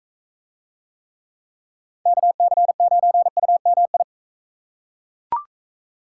Morse code, heard as a keyed tone, about 42 wpm, 700 Hz. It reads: KC9UMI